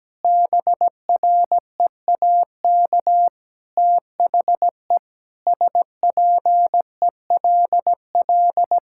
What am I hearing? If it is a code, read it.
Morse code, heard as a keyed tone, about 17 wpm, 700 Hz. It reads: BREAK THE SPELL
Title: Sentences from Top 1000 Words - Review Part 6 - 17wpm